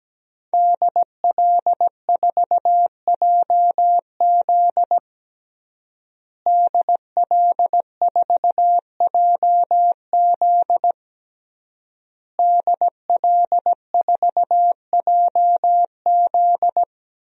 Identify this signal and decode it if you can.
Morse code, heard as a keyed tone, about 17 wpm, 700 Hz. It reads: DL4JZ DL4JZ DL4JZ
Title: DXCC-100 Call Signs - 5 Characters - Repeated 3x - 17wpm